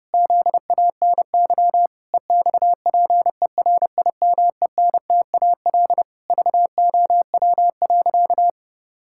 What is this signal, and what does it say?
Morse code, heard as a keyed tone, about 28 wpm, 700 Hz. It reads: ZANY EXPERIMENTAL VOW.
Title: Pangram Challenge v2 - 30wpm